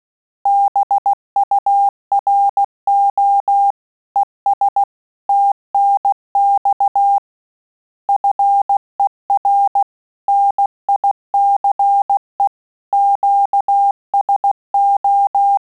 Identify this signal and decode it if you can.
Morse code, heard as a keyed tone, about 16 wpm, 800 Hz. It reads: BURO ES TNX FER NICE QSO
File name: QSO CW 75 CPM